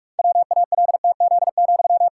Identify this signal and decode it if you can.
Morse code, heard as a keyed tone, about 46 wpm, 700 Hz. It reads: WALTZ,
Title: Pangram Challenge v2 - 45wpm